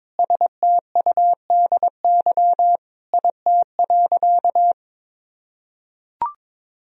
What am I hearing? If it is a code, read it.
Morse code, heard as a keyed tone, about 22 wpm, 700 Hz. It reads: STUDY IT.
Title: Sentences from Top 1000 Words - Review Part 1 - 22wpm